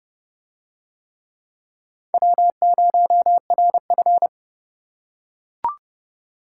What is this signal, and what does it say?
Morse code, heard as a keyed tone, about 29 wpm, 700 Hz. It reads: W0RF